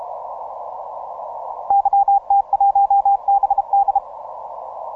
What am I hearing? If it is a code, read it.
Morse code, heard as a keyed tone, about 32 wpm, 800 Hz. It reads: YT1BD
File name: simple-file-list / H44WA 10MCW